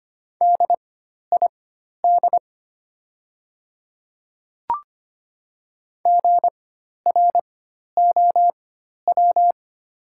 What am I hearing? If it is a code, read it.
Morse code, heard as a keyed tone, about 25 wpm, spaced as for 12 wpm, 700 Hz. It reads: DID GROW
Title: 4x Character Spacing - Top 300 Words - 25wpm